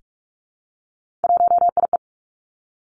Morse code, heard as a keyed tone, about 46 wpm, 700 Hz. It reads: 1IE
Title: Three Challenging Letter-Number Combinations 45wpm